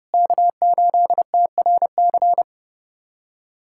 Morse code, heard as a keyed tone, about 30 wpm, 700 Hz. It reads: K8TRC
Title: USA Callsign DE Callsign - 5 Characters - 30wpm